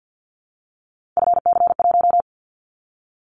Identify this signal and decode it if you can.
Morse code, heard as a keyed tone, about 50 wpm, 700 Hz. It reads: FC1
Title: Callsign Prefixes for Top 50 DXCC - 50wpm